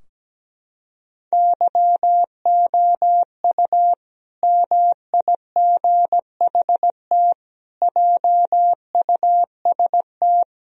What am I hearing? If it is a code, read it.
Morse code, heard as a keyed tone, about 17 wpm, 700 Hz. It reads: YOU MIGHT JUST
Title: Sentences from Top 1000 Words - Review Part 2 - 17wpm